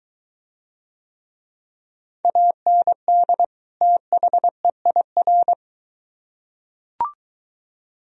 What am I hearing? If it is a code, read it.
Morse code, heard as a keyed tone, about 23 wpm, 700 Hz. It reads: AND THEIR